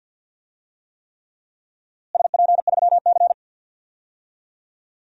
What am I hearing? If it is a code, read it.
Morse code, heard as a keyed tone, about 52 wpm, 700 Hz. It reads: SP3C
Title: DXCC-100 Call Signs - 4 Characters - Repeated 3x - 50wpm